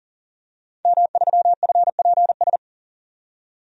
Morse code, heard as a keyed tone, about 40 wpm, 700 Hz. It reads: M3FPS
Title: DXCC-100 Call Signs - 5 Characters - 40wpm